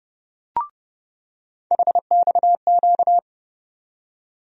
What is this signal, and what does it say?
Morse code, heard as a keyed tone, about 30 wpm, 700 Hz. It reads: HXQ
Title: USA Call Sign Postfixes - 3 Letters 30wpm